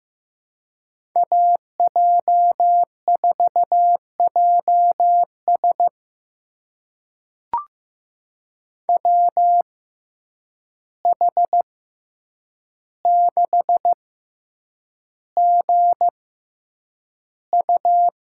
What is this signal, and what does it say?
Morse code, heard as a keyed tone, about 15 wpm, spaced as for 8 wpm, 700 Hz. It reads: AJ4JS WH6GU
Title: USA Call Signs - 5 Characters - 6x Character Spacing - 15wpm